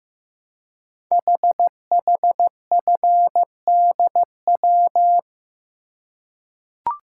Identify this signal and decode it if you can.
Morse code, heard as a keyed tone, about 15 wpm, 690 Hz. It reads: HHFDW
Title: Code Groups - 5 Letters - 15wpm